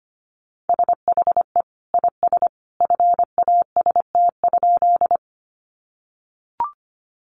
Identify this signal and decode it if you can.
Morse code, heard as a keyed tone, about 25 wpm, 700 Hz. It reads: SHE IS FAST?
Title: Sentences from Top 400 Words 25wpm